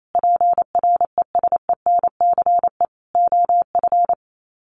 Morse code, heard as a keyed tone, about 28 wpm, 700 Hz. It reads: PRESENCE OF
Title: Sets of 3 Words 28wpm